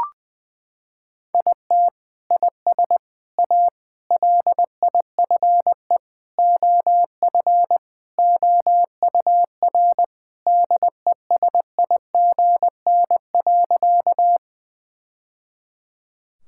Morse code, heard as a keyed tone, about 21 wpm, 700 Hz. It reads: IT IS A LIFE OF OUR DESIGN.
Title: Sentences from Top 1000 Words - Review Part 5 - 20wpm